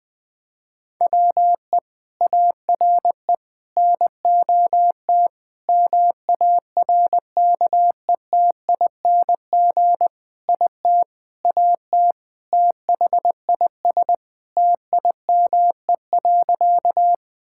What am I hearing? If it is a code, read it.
Morse code, heard as a keyed tone, about 20 wpm, 700 Hz. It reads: WE ARE NOT MARKETING IT AT THIS TIME.
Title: Sentences from Top 1000 Words - Review Part 5 - 20wpm